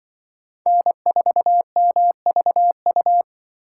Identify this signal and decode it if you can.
Morse code, heard as a keyed tone, about 23 wpm, 700 Hz. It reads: N4MVU
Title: Speed Racing - USA 1x3 Call Signs 20wpm